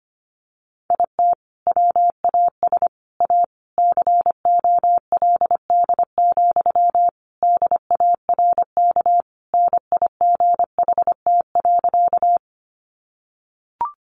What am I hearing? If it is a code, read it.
Morse code, heard as a keyed tone, about 25 wpm, 700 Hz. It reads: IT WAS A COLD, DARK NIGHT.
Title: Sentences from Top 500 Words 25wpm